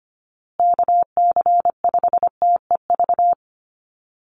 Morse code, heard as a keyed tone, about 25 wpm, 700 Hz. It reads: KC5TEV
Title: USA Call Signs Repeated 2x 25wpm